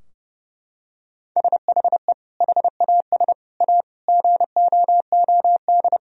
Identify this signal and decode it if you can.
Morse code, heard as a keyed tone, about 30 wpm, 700 Hz. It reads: SHE HAS A GOOD
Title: Sentences from Top 1000 Words - Review Part 4 - 30wpm